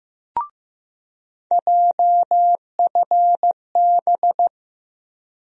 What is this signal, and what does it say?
Morse code, heard as a keyed tone, about 15 wpm, 700 Hz. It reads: JFB
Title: USA Call Sign Postfixes - 3 Letters 15wpm